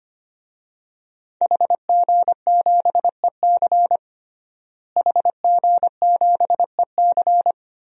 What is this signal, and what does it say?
Morse code, heard as a keyed tone, about 25 wpm, 700 Hz. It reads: HG7EC HG7EC
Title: DXCC-100 Call Signs - 5 Characters - Repeated 3x - 25wpm